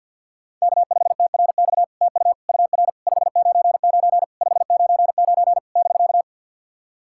Morse code, heard as a keyed tone, about 53 wpm, 690 Hz. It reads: K5TRX TU UR 599 599 <BK>
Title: SOTA QSO Practice 50wpm